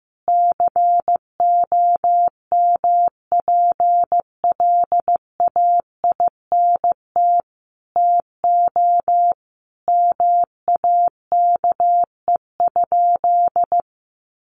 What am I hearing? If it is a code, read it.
Morse code, heard as a keyed tone, about 15 wpm, 700 Hz. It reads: COMPLAINT TO MAKE?
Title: Sentences from Top 1400 Words 15wpm